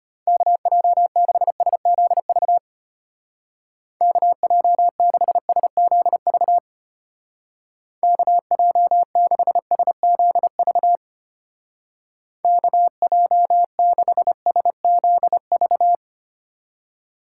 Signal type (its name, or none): Morse code